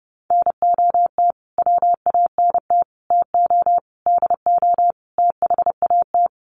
Morse code, heard as a keyed tone, about 30 wpm, 700 Hz. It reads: NOT WANT TO DO THAT
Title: Sets of 5 Words Encore 30wpm